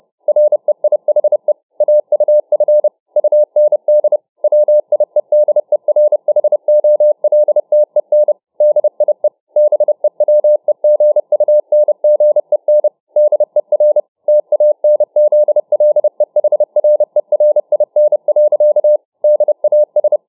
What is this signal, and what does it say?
Morse code, heard as a keyed tone, about 29 wpm, 600 Hz. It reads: REIHE AUF UND WIEDERHOLTEN DIE BEWEGUNGEN DER TANZLEHRERIN. DAS